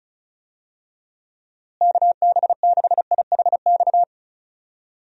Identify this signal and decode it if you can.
Morse code, heard as a keyed tone, about 36 wpm, 700 Hz. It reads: KB6IHX
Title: USA Call Signs - 6 Characters - Repeated 3x - 35wpm